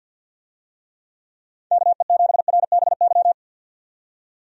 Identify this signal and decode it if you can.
Morse code, heard as a keyed tone, about 47 wpm, 700 Hz. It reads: KE7RBY
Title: USA Call Signs - 6 Characters - 50wpm